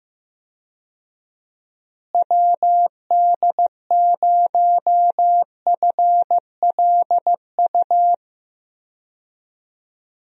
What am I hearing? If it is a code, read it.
Morse code, heard as a keyed tone, about 15 wpm, 700 Hz. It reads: WD0FLU
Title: USA Call Signs - 6 Characters - Repeated 3x - 15wpm